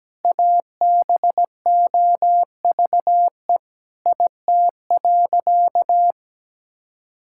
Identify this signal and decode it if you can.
Morse code, heard as a keyed tone, about 17 wpm, 700 Hz. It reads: ABOVE IT.
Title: Sentences from Top 1600 Words 17wpm